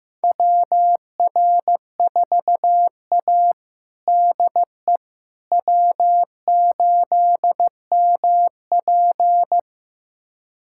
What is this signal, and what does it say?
Morse code, heard as a keyed tone, about 15 wpm, 700 Hz. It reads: WR4A DE W8MP
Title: USA Callsign DE Callsign - 4 Characters - 15wpm